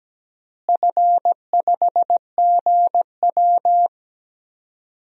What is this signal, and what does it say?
Morse code, heard as a keyed tone, about 17 wpm, 700 Hz. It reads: F5GW
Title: DXCC-100 Call Signs - 4 Characters - Repeated 2x - 17wpm